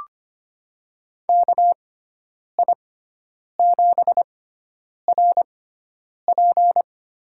Morse code, heard as a keyed tone, about 25 wpm, spaced as for 9 wpm, 700 Hz. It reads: KI7RP
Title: USA Call Signs - 6x Character Spacing - 25wpm